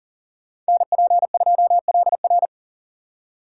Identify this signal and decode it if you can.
Morse code, heard as a keyed tone, about 49 wpm, 700 Hz. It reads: NP2LR
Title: Speed Racing - USA 2x2 Call Signs 40wpm